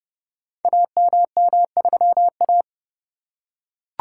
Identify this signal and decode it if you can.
Morse code, heard as a keyed tone, about 30 wpm, 700 Hz. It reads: AMM3A
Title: Code Groups - 5 Letter:Numbers - 30wpm